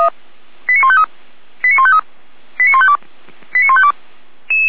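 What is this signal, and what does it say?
Five-tone selective call (ZVEI), heard as quick runs of five tones, tones of about 70 ms. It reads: 89153 89153 89152 89152
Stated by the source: ZVEI five-tone selective call 89165 received on an SDR.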